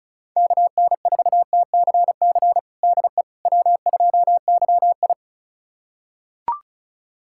Morse code, heard as a keyed tone, about 36 wpm, 700 Hz. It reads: KN4TCC DE W2YI
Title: USA Callsign DE Callsign 35wpm